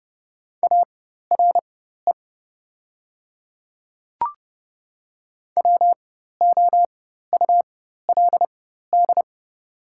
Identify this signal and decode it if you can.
Morse code, heard as a keyed tone, about 30 wpm, spaced as for 14 wpm, 700 Hz. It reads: ARE WOULD
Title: 4x Character Spacing - Top 100 Words - 30wpm